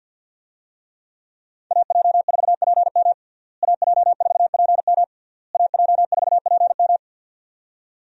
Morse code, heard as a keyed tone, about 51 wpm, 700 Hz. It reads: AJ4PM AJ4PM AJ4PM
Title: USA Call Signs - 5 Characters - Repeated 3x - 50wpm